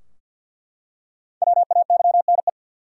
Morse code, heard as a keyed tone, about 48 wpm, 700 Hz. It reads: WAYNE
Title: USA Common Names - 50wpm